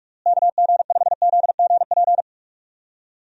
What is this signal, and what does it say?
Morse code, heard as a keyed tone, about 46 wpm, 700 Hz. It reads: KG5ZGP
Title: USA Call Signs - 6 Characters - Repeated 3x - 45wpm